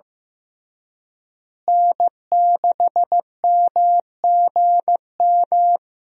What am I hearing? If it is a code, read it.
Morse code, heard as a keyed tone, about 15 wpm, 700 Hz. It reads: N6MGM
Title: USA Call Signs - 5 Characters - Repeated 3x - 15wpm